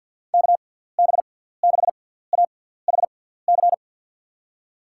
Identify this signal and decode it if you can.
Morse code, heard as a keyed tone, about 54 wpm, spaced as for 17 wpm, 700 Hz. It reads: KB6AHC
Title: USA Call Signs - 6 Characters - 6x Character Spacing - 50wpm